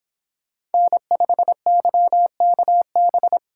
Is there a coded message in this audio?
Yes: Morse code from 0.7 to 3.4 s, heard as a keyed tone, about 30 wpm, 700 Hz. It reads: N5YKB